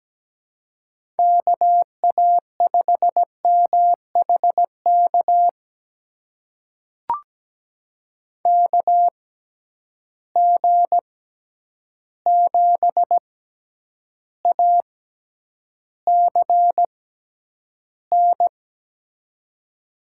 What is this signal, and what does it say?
Morse code, heard as a keyed tone, about 17 wpm, spaced as for 9 wpm, 700 Hz. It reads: KA5MHK KG7ACN
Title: USA Call Signs - 6 Characters - 6x Character Spacing - 17wpm